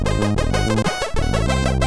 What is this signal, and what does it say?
There is no code.